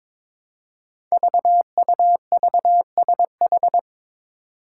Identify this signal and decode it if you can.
Morse code, heard as a keyed tone, about 22 wpm, 700 Hz. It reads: VUVSH
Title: Five Challenging Letter-Number Combinations 22wpm